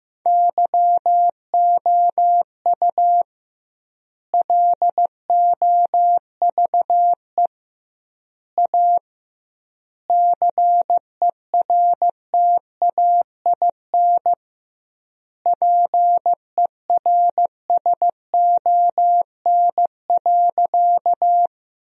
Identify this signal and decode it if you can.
Morse code, heard as a keyed tone, about 15 wpm, 700 Hz. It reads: YOU LOVE A CERTAIN PERSON.
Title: 2x Word-Spacing - Sentences from Top 500 Words - 15wpm